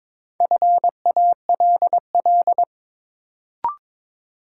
Morse code, heard as a keyed tone, about 21 wpm, 700 Hz. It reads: FALL
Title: Sentences from Top 500 Words - New Word Drill - 22wpm